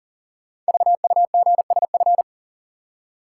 Morse code, heard as a keyed tone, about 39 wpm, 700 Hz. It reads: VUGSF
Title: Code Groups - 5 Letters - 40wpm